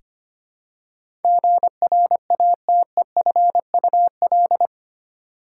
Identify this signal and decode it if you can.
Morse code, heard as a keyed tone, about 25 wpm, 700 Hz. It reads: GRATEFUL
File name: Eight Letter Words - 25wpm